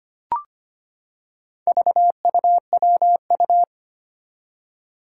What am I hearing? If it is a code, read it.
Morse code, heard as a keyed tone, about 25 wpm, 700 Hz. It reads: VUWU
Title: Four Challenging Letter-Number Combinations 25wpm